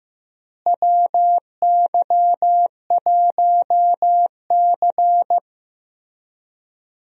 Morse code, heard as a keyed tone, about 15 wpm, 700 Hz. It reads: WY1C
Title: USA Call Signs - 4 Characters - Repeated 3x - 15wpm